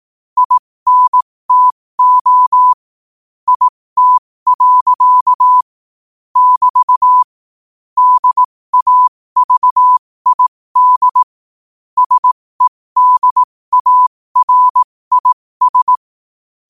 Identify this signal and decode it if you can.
Morse code, heard as a keyed tone, about 18 wpm, spaced as for 15 wpm, 1000 Hz. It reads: INTO IT. <BT> DAVID SEDARIS
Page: qotd-at-15-wpm